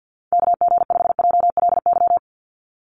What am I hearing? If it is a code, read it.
Morse code, heard as a keyed tone, about 50 wpm, 700 Hz. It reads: KG5JLY
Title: USA Call Signs 50wpm